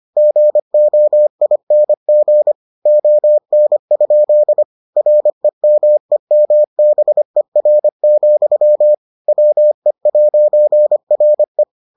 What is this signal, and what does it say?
Morse code, heard as a keyed tone, about 25 wpm, 600 Hz. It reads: GOING ON? REMEMBER, WE'RE